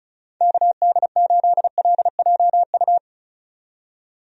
Morse code, heard as a keyed tone, about 34 wpm, 700 Hz. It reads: KD8LJU
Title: USA Call Signs - 6 Characters - 35wpm